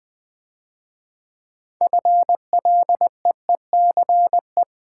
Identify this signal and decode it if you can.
Morse code, heard as a keyed tone, about 20 wpm, 700 Hz. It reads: FLEECE
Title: Difficult Words - Speed-Racing with Hints - 20wpm